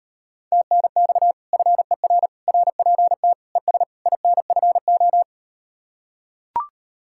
Morse code, heard as a keyed tone, about 38 wpm, 700 Hz. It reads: TNX FER RPT ES INFO